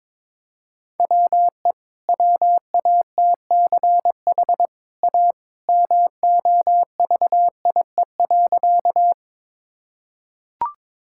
Morse code, heard as a keyed tone, about 22 wpm, 700 Hz. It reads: WE WATCH A MOVIE.
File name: practice / 22wpm / Easy Sentences with Hints - 22wpm